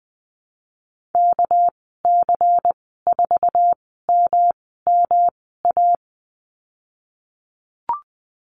Morse code, heard as a keyed tone, about 20 wpm, spaced as for 15 wpm, 700 Hz. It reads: KC4MMA
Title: 213 Lesson 34g - Number 8 - Call Signs 20wpm